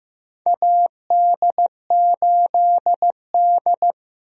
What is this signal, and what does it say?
Morse code, heard as a keyed tone, about 16 wpm, 700 Hz. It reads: AD8D
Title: USA Call Signs - 4 Characters - Repeated 2x - 15wpm